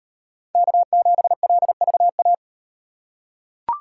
Morse code, heard as a keyed tone, about 37 wpm, 700 Hz. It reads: K7LVA